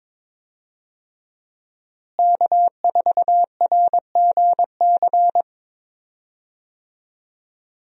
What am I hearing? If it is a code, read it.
Morse code, heard as a keyed tone, about 22 wpm, 700 Hz. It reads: K4RGC
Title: USA Call Signs - 5 Characters - Repeated 2x - 22wpm